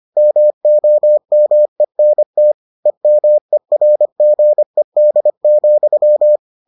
Morse code, heard as a keyed tone, about 25 wpm, 600 Hz. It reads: MOMENT EMERGED,